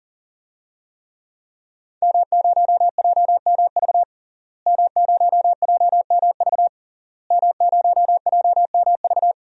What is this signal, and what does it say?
Morse code, heard as a keyed tone, about 40 wpm, 700 Hz. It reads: M0JMV M0JMV M0JMV
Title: DXCC-100 Call Signs - 5 Characters - Repeated 3x - 40wpm